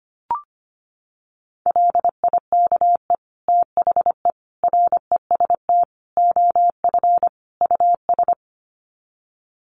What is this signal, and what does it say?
Morse code, heard as a keyed tone, about 25 wpm, 700 Hz. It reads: LIKE THE REST OF US
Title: Sets of 5 Words Encore2 25wpm